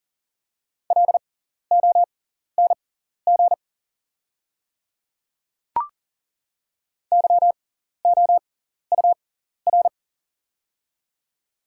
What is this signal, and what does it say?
Morse code, heard as a keyed tone, about 40 wpm, spaced as for 14 wpm, 700 Hz. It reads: LONG YOUR
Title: 6x Character Spacing - Top 100 Words - 40wpm